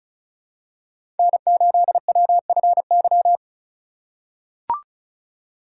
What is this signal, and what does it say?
Morse code, heard as a keyed tone, about 36 wpm, 700 Hz. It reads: N8WFY
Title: USA Call Signs - 5 Characters - 4x Character Spacing - 35wpm